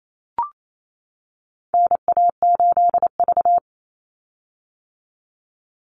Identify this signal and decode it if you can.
Morse code, heard as a keyed tone, about 28 wpm, 700 Hz. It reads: NA8V
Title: Call Signs - 28wpm